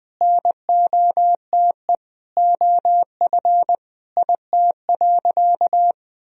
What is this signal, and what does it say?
Morse code, heard as a keyed tone, about 20 wpm, 700 Hz. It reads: NOTE OF IT.
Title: Sentences from Top 500 Words and Abbreviations 20wpm